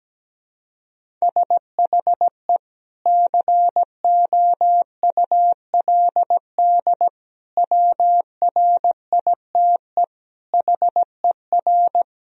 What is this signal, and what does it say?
Morse code, heard as a keyed tone, about 17 wpm, 700 Hz. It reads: SHE COULD WRITE HER
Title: Sentences from Top 1000 Words - Review Part 1 - 17wpm